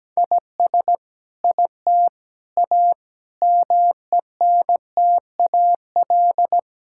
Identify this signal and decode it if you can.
Morse code, heard as a keyed tone, about 17 wpm, 700 Hz. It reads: IS IT A MENTAL
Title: Sentences from Top 1000 Words - Review Part 6 - 17wpm